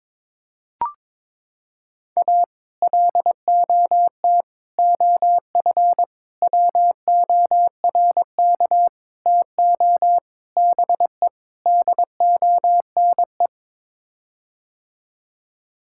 Morse code, heard as a keyed tone, about 22 wpm, 700 Hz. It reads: A LOT OF WORK TO BE DONE
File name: practice / 22wpm / Sets of 7 Words - 22wpm